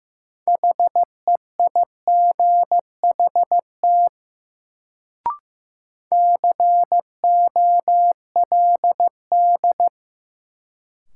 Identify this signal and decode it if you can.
Morse code, heard as a keyed tone, about 15 wpm, 700 Hz. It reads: HEIGHT COLD
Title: Sets of 5 Words - New Word Drill - 15wpm